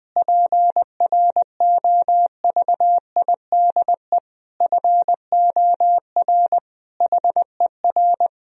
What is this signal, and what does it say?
Morse code, heard as a keyed tone, about 20 wpm, 700 Hz. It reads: PROVIDE FOR HER
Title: Sentences from Top 1000 Words - Review Part 3 - 20wpm